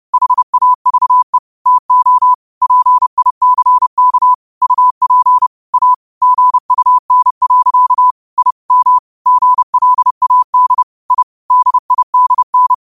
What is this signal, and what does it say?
Morse code, heard as a keyed tone, about 30 wpm, 1000 Hz. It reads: HAVE TO PICK UP A GUN. I M GLAD I DIDN
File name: qotdfiles / 30wpm